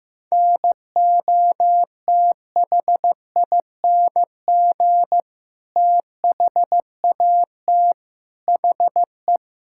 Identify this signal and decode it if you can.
Morse code, heard as a keyed tone, about 15 wpm, 700 Hz. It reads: NOTHING THAT HE
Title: Sentences from Top 1000 Words - Review Part 6 - 15wpm